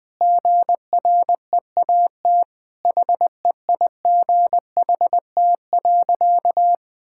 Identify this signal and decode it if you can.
Morse code, heard as a keyed tone, about 21 wpm, 700 Hz. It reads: GREAT HEIGHT.
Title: Sentences from Top 1600 Words 20wpm